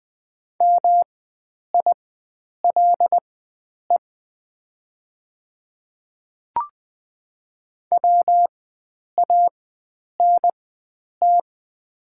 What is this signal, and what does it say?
Morse code, heard as a keyed tone, about 20 wpm, spaced as for 9 wpm, 700 Hz. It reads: MILE WANT
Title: 4x Character Spacing - Top 500 Words - 20wpm